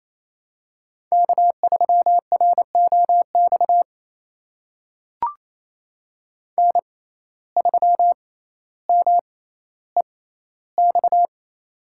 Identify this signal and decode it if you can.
Morse code, heard as a keyed tone, about 28 wpm, spaced as for 14 wpm, 700 Hz. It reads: K3ROX N3MEX
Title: USA Call Signs - 5 Characters - 6x Character Spacing - 28wpm